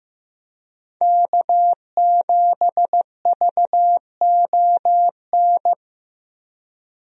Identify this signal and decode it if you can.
Morse code, heard as a keyed tone, about 15 wpm, 700 Hz. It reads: K7VON
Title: USA Call Signs - 5 Characters - Repeated 2x - 15wpm